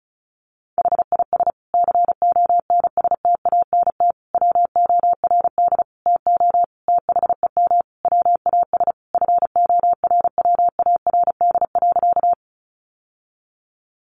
Morse code, heard as a keyed tone, about 34 wpm, 700 Hz. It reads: HIS CONSTANT WORD TO THEM WAS FORWARD.
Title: Sentences from Top 1500 Words 35wpm